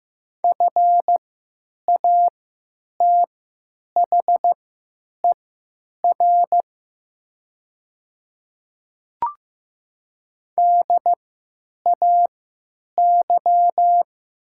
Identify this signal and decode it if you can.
Morse code, heard as a keyed tone, about 15 wpm, spaced as for 9 wpm, 700 Hz. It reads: FATHER DAY
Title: 3x Character Spacing - Top 300 Words - 15wpm